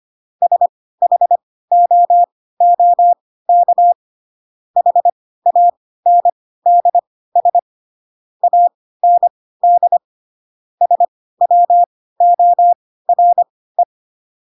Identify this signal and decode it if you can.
Morse code, heard as a keyed tone, about 25 wpm, spaced as for 16 wpm, 700 Hz. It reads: SHOOK HANDS AND SWORE
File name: Pinocchio / Pinocchio ch2 10 25 ik1hge